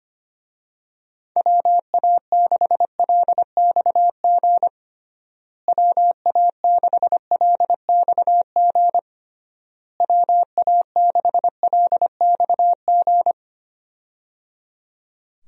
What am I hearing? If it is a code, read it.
Morse code, heard as a keyed tone, about 25 wpm, 700 Hz. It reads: WA6LXG WA6LXG WA6LXG
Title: USA Call Signs - 6 Characters - Repeated 3x - 25wpm